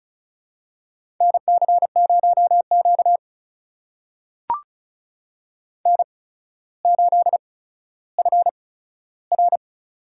Morse code, heard as a keyed tone, about 36 wpm, spaced as for 15 wpm, 700 Hz. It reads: NC0Q N8FR
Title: USA Call Signs - 4 Characters - 8x Character Spacing - 35wpm